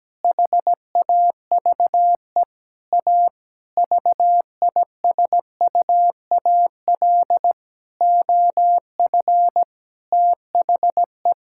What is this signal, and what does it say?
Morse code, heard as a keyed tone, about 17 wpm, 700 Hz. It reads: HAVE A VISUAL OF THE
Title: Sentences from Top 1000 Words - Review Part 6 - 17wpm